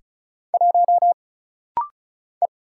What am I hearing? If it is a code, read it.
Morse code, heard as a keyed tone, about 36 wpm, 700 Hz. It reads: 1 E
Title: Single Letter-Number - Rapid-Fire 35wpm